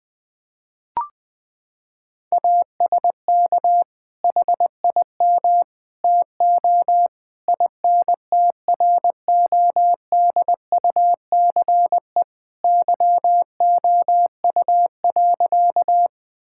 Morse code, heard as a keyed tone, about 21 wpm, 700 Hz. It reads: ASK HIM TO INTRODUCE YOU.
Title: Sentences from Top 1600 Words 20wpm